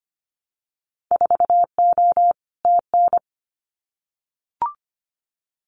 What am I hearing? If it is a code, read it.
Morse code, heard as a keyed tone, about 25 wpm, 700 Hz. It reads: 4O TN
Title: Winter Field Day Exchanges Encore 25wpm